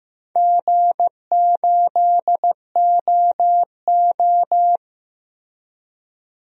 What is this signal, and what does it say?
Morse code, heard as a keyed tone, about 15 wpm, 700 Hz. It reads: G8OO
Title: DXCC-100 Call Signs - 4 Characters - Repeated 2x - 15wpm